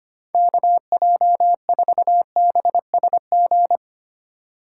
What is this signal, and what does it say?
Morse code, heard as a keyed tone, about 25 wpm, 700 Hz. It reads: KJ4BSG
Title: USA Call Signs - 4x Character Spacing - 25wpm